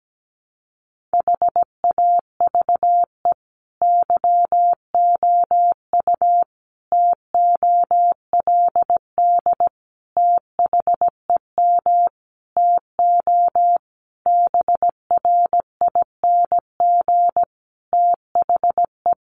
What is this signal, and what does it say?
Morse code, heard as a keyed tone, about 17 wpm, 700 Hz. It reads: HAVE YOU TOLD THEM TO BRING THE
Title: Sentences from Top 500 Words 17wpm